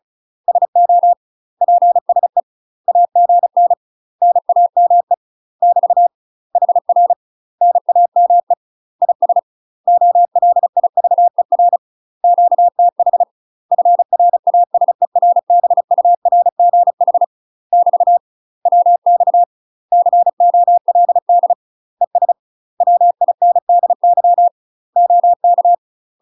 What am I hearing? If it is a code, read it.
Morse code, heard as a keyed tone, about 36 wpm, 700 Hz. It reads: SO PSE AGN NAME <BT> HR NAME IS OLIVER QTH FRASERBURGH <BT> WX COLD ES WINDY OK
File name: UK MASTER QSOs TRAINING / 35WPM / UK MASTER QSOs 0028